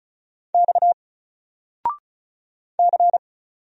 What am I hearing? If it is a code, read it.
Morse code, heard as a keyed tone, about 35 wpm, 700 Hz. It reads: X C
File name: Single-Letter-Number-35wpm